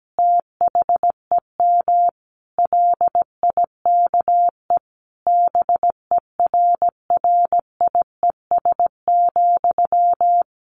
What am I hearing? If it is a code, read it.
Morse code, heard as a keyed tone, about 17 wpm, 700 Hz. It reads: THEM LIKE BERRIES,
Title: A Sound Of Thunder 17wpm